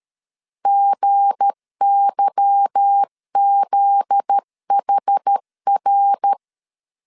Morse code, heard as a keyed tone, about 13 wpm, 800 Hz. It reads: GYZHR
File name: LEZIONI / 8 YJ45 / 8 YJ45 12wpm 2151